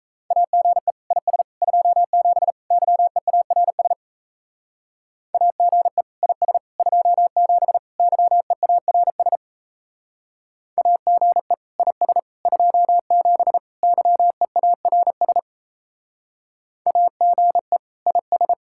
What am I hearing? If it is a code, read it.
Morse code, heard as a keyed tone, about 36 wpm, 700 Hz. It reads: AGE IS 27 YEARS AGE IS 27 YEARS AGE IS 27 YEARS AGE IS